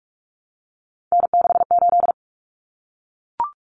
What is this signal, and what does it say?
Morse code, heard as a keyed tone, about 44 wpm, 700 Hz. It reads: N68
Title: Common Cut Numbers - 3-Digit 45wpm